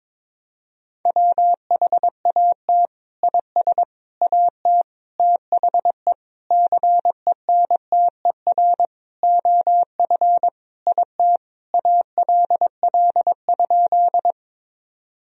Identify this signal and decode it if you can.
Morse code, heard as a keyed tone, about 21 wpm, 700 Hz. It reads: WHAT IS AT THE CENTER OF IT ALL?
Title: Sentences from Top 1000 Words - Review Part 2 - 22wpm